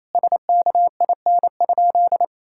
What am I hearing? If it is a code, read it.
Morse code, heard as a keyed tone, about 28 wpm, 700 Hz. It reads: SKIN?
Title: Sentences from Top 1600 Words 28wpm